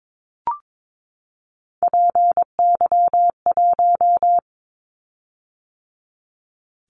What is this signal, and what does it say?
Morse code, heard as a keyed tone, about 22 wpm, 700 Hz. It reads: PY1
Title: Callsign Prefixes for Top 100 DXCC - 22wpm